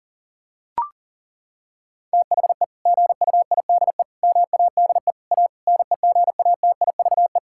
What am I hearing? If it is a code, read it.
Morse code, heard as a keyed tone, about 37 wpm, 700 Hz. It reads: THE GUIDE MADE A NEGATIVE
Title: Sentences from Top 1000 Words - Review Part 6 - 40wpm